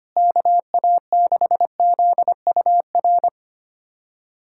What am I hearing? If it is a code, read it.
Morse code, heard as a keyed tone, about 25 wpm, 700 Hz. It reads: KA6ZUR
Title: USA Callsign DE Callsign - 6 Characters - 25wpm